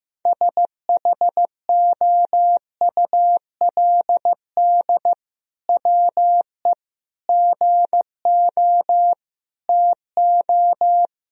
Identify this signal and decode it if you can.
Morse code, heard as a keyed tone, about 15 wpm, 700 Hz. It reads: SHOULD WE GO TO
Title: Sentences from Top 1000 Words - Review Part 3 - 15wpm